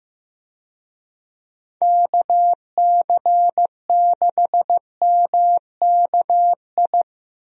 Morse code, heard as a keyed tone, about 15 wpm, 700 Hz. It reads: KC6MKI
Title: USA Call Signs - 6 Characters - 15wpm